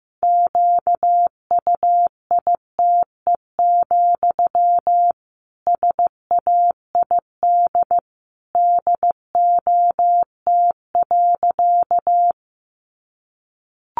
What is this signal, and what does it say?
Morse code, heard as a keyed tone, about 15 wpm, 700 Hz. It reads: QUITE, SAID DOT.
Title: Sentences from Top 1500 Words 15wpm